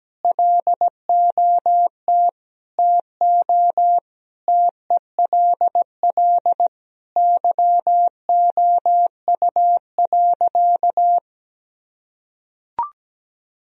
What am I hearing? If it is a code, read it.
Morse code, heard as a keyed tone, about 17 wpm, 700 Hz. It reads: LOT TO TELL YOU.
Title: Sentences from Top 1000 Words - Review Part 3 - 17wpm